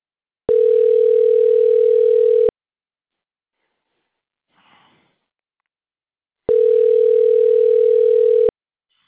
Ringback tone (North America), 440 + 480 Hz.